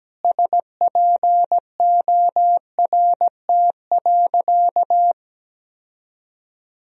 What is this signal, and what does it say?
Morse code, heard as a keyed tone, about 17 wpm, 700 Hz. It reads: SPORT.
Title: Sentences from Top 1000 Words - Review Part 4 - 17wpm